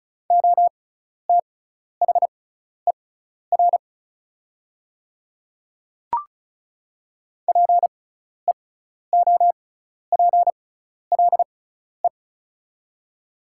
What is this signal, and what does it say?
Morse code, heard as a keyed tone, about 36 wpm, spaced as for 12 wpm, 700 Hz. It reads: OTHER PEOPLE
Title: 6x Character Spacing - Top 100 Words - 35wpm